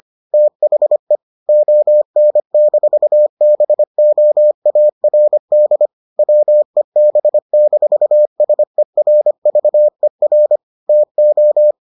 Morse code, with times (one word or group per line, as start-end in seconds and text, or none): 0.3-1.2 THE
1.5-5.9 ON-BOARD
6.2-10.6 WEB-SERVER
10.9-11.7 TO